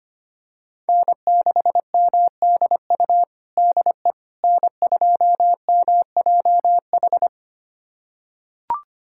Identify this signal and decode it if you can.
Morse code, heard as a keyed tone, about 25 wpm, 700 Hz. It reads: N6MDU DE N2MJH
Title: USA Callsign DE Callsign - 5 Characters - 25wpm